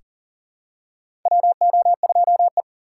Morse code, heard as a keyed tone, about 40 wpm, 700 Hz. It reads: WO2E
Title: K1USN SST QSO Practice - 40wpm